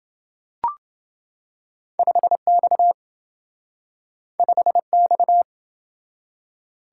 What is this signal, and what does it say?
Morse code, heard as a keyed tone, about 27 wpm, 700 Hz. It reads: 5X 5X